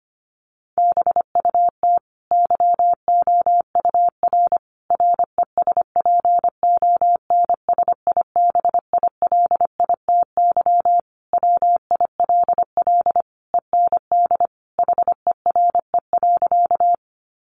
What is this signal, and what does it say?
Morse code, heard as a keyed tone, about 25 wpm, 700 Hz. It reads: BUT YOUR RESPONSIBILITY WILL END HERE.
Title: Sentences from Top 1200 Words 25wpm